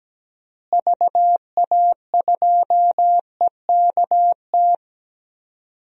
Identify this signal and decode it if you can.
Morse code, heard as a keyed tone, about 17 wpm, 700 Hz. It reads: VA2EKT
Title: DXCC-100 Call Signs - 6 Characters - Repeated 2x - 17wpm